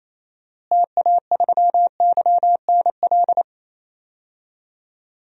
Morse code, heard as a keyed tone, about 28 wpm, 700 Hz. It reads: TA3YNL
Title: DXCC-100 Call Signs - 6 Characters - Repeated 2x - 28wpm